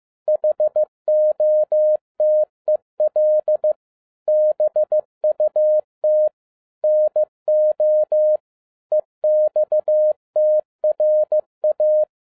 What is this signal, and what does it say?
Morse code, heard as a keyed tone, about 15 wpm, 610 Hz. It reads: HOTEL BUT NO EXTRA
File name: 2011-07-mp3 / n12 T8XO 15wpm